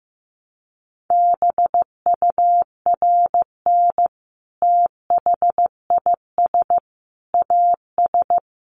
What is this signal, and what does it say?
Morse code, heard as a keyed tone, about 15 wpm, 700 Hz. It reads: BURN THIS AS
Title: Sentences from Top 1300 Words 15wpm